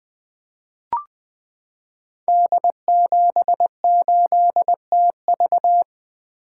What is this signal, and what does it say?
Morse code, heard as a keyed tone, about 20 wpm, 700 Hz. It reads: D78TV
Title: Code Groups - 5 Letter:Numbers - 20wpm